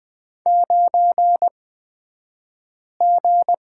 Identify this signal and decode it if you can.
Morse code, heard as a keyed tone, about 20 wpm, spaced as for 5 wpm, 700 Hz. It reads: 9G
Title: 228 Lesson 37d - Letter x - Two Characters 20wpm